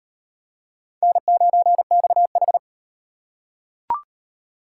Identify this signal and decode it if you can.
Morse code, heard as a keyed tone, about 33 wpm, 700 Hz. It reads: N9XH